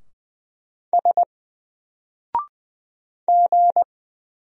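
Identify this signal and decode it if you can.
Morse code, heard as a keyed tone, about 20 wpm, 700 Hz. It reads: S G
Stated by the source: Single letter and number Morse drill at 20 wpm, from N4LCD.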